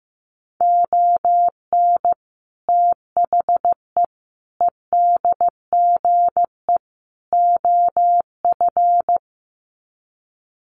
Morse code, heard as a keyed tone, about 15 wpm, 700 Hz. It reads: ON THE EDGE OF
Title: Sets of 4 Words 15wpm